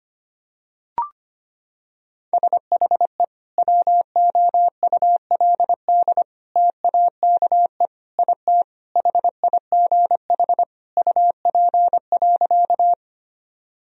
Morse code, heard as a keyed tone, about 25 wpm, 700 Hz. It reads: SHE WOULD TAKE IT HIGH UP.
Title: Sentences from Top 1000 Words - Review Part 1 - 25wpm